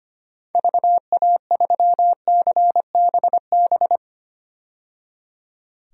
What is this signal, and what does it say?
Morse code, heard as a keyed tone, about 25 wpm, 700 Hz. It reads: VA3CBB
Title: DXCC-100 Call Signs - 6 Characters - 25wpm